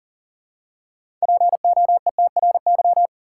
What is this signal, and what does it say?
Morse code, heard as a keyed tone, about 40 wpm, 700 Hz. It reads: POETRY
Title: Six Letter Words - 40wpm